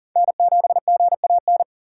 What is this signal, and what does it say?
Morse code, heard as a keyed tone, about 40 wpm, 700 Hz. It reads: N7GAN
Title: USA Call Signs - 6x Character Spacing - 40wpm